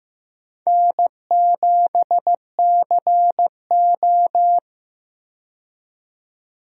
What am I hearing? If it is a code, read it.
Morse code, heard as a keyed tone, about 16 wpm, 700 Hz. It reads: N7CO
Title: USA Call Signs - 4 Characters - 15wpm